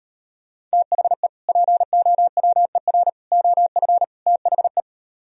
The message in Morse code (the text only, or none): THE POWER OF THE